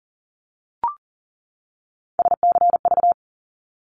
Morse code, heard as a keyed tone, about 40 wpm, 700 Hz. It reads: SCV
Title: ARRL Sections 40wpm